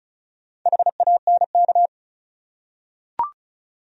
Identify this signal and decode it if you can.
Morse code, heard as a keyed tone, about 36 wpm, 700 Hz. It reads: HANK
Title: USA Common Names - 35wpm